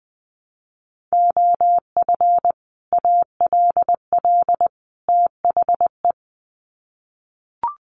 Morse code, heard as a keyed tone, about 20 wpm, 700 Hz. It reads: OF ALL THE
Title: Sets of 3 Words 20wpm